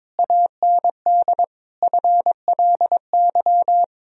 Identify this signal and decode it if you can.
Morse code, heard as a keyed tone, about 22 wpm, 700 Hz. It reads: AND FLY
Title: Sentences from Top 1000 Words - Review Part 2 - 22wpm